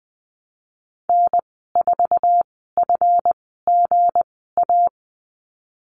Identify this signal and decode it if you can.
Morse code, heard as a keyed tone, about 20 wpm, spaced as for 15 wpm, 700 Hz. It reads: N4FGA
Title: 167 Lesson 27g - Number 7 - Call Signs 20wpm